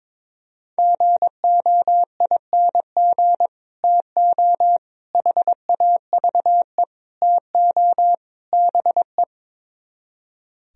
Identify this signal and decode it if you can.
Morse code, heard as a keyed tone, about 23 wpm, 700 Hz. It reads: GOING TO HAVE TO BE
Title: Speed-Racing - Sets of 5 Words 22wpm